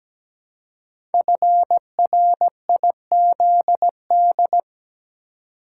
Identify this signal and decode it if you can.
Morse code, heard as a keyed tone, about 17 wpm, 700 Hz. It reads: FRIZD
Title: Code Groups - 5 Letter:Numbers - 17wpm